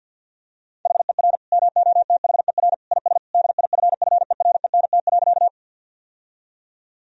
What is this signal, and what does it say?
Morse code, heard as a keyed tone, about 50 wpm, 700 Hz. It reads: HER MOTHER IS DIFFERENT.
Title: Sentences from Top 1000 Words - Review Part 3 - 50wpm